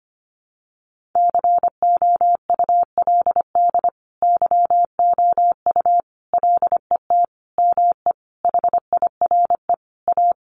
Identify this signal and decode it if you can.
Morse code, heard as a keyed tone, about 25 wpm, 700 Hz. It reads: COULD YOU LET ME HIRE A
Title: Sentences from Top 1400 Words 25wpm